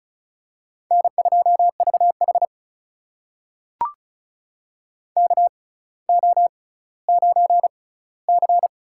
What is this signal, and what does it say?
Morse code, heard as a keyed tone, about 36 wpm, spaced as for 18 wpm, 700 Hz. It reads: N2VH KO9C
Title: USA Call Signs - 4 Characters - 6x Character Spacing - 35wpm